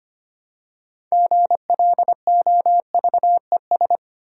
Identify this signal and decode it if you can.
Morse code, heard as a keyed tone, about 25 wpm, 700 Hz. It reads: GLOVES
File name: practice / 25wpm / Easy Words with Hints - 25wpm